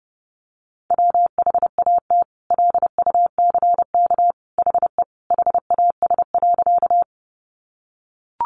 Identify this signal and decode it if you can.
Morse code, heard as a keyed tone, about 28 wpm, 700 Hz. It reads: WHAT LUCK HE HAS.
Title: Sentences from Top 1500 Words 30wpm